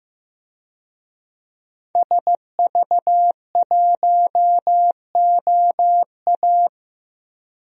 Morse code, heard as a keyed tone, about 15 wpm, 700 Hz. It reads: SV1OA
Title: DXCC-100 Call Signs - 5 Characters - Repeated 3x - 15wpm